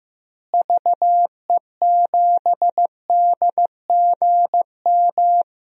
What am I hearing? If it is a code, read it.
Morse code, heard as a keyed tone, about 15 wpm, 700 Hz. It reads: VE7DGM
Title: DXCC-100 Call Signs - 6 Characters - Repeated 2x - 15wpm